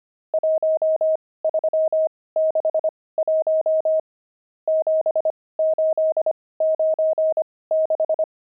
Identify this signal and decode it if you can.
Morse code, heard as a keyed tone, about 25 wpm, spaced as for 18 wpm, 630 Hz. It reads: 1361 7896
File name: pangram-fs-18-cs-25-nums4